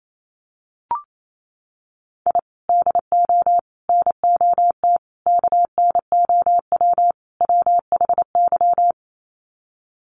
Morse code, heard as a keyed tone, about 28 wpm, 700 Hz. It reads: I DO NOT KNOW WHY
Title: Sets of 5 Words 28wpm